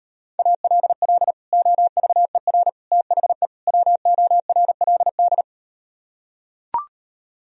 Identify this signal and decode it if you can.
Morse code, heard as a keyed tone, about 38 wpm, 700 Hz. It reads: ALL OVER THE WORLD